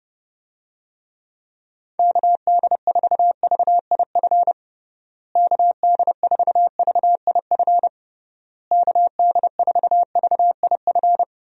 Morse code, heard as a keyed tone, about 30 wpm, 700 Hz. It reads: KD4VIF KD4VIF KD4VIF
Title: USA Call Signs - 6 Characters - Repeated 3x - 30wpm